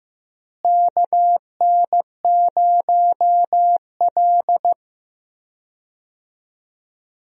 Morse code, heard as a keyed tone, about 15 wpm, 700 Hz. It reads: KN0L
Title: USA Call Signs - 4 Characters - Repeated 3x - 15wpm